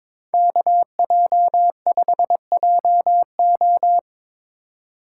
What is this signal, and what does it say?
Morse code, heard as a keyed tone, about 25 wpm, 700 Hz. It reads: KJ5JO
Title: Speed Racing - USA 2x2 Call Signs 22wpm